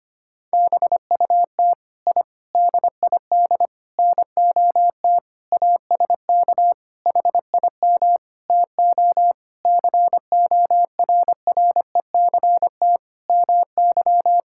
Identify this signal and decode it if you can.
Morse code, heard as a keyed tone, about 25 wpm, 700 Hz. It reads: BUT I DID NOT ASK HIM TO CORRECT MY
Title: Sentences from Top 1000 Words - Review Part 5 - 25wpm